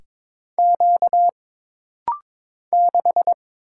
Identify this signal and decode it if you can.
Morse code, heard as a keyed tone, about 22 wpm, 700 Hz. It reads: Q 6
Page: This rapid-fire timing is geared to push your ICR (Instant Character Recognition) skills.